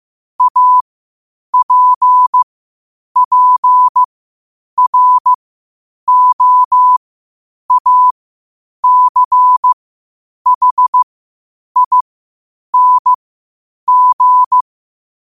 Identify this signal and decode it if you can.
Morse code, heard as a keyed tone, about 15 wpm, spaced as for 8 wpm, 1000 Hz. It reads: APPROACHING
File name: qotdfiles / 8wpm